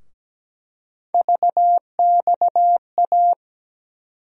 Morse code, heard as a keyed tone, about 17 wpm, 700 Hz. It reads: VXA
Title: USA Call Sign Postfixes - 3 Letters 17wpm